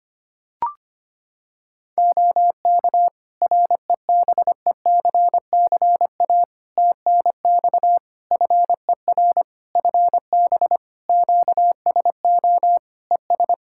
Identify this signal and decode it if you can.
Morse code, heard as a keyed tone, about 25 wpm, 700 Hz. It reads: OK REBECCA TNX FER FB QSO ES
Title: Practice QSO - Exchange 4 - Ending - 25wpm